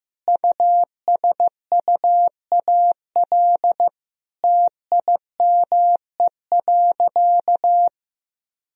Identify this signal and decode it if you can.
Morse code, heard as a keyed tone, about 15 wpm, 700 Hz. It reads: USUAL TIME.
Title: Sentences from Top 1000 Words - Review Part 2 - 15wpm